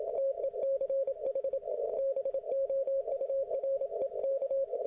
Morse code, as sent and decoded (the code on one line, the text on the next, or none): - . -.-. .... / -... --- ..- .-. -.-
TECH BOUREK